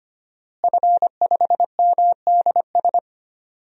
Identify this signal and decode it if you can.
Morse code, heard as a keyed tone, about 25 wpm, 700 Hz. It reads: F5MDS
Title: DXCC-100 Call Signs - 5 Characters - Repeated 3x - 25wpm